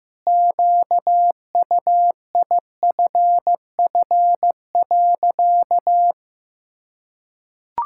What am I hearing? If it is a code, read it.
Morse code, heard as a keyed tone, about 15 wpm, 700 Hz. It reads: QUIFF.
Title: Pangram Challenge v2 - 15wpm